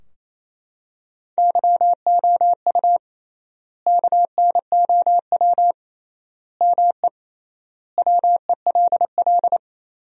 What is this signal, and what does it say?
Morse code, heard as a keyed tone, about 28 wpm, 700 Hz. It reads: YOU KNOW ME WELL
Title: 3x Word-Spacing - Sentences from Top 400 Words - 28wpm